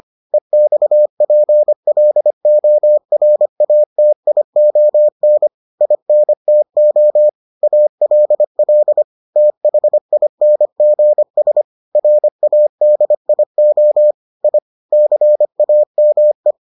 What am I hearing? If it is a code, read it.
Morse code, heard as a keyed tone, about 25 wpm, 600 Hz. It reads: EXPLORATION INTO ALL THINGS RADIO I CAME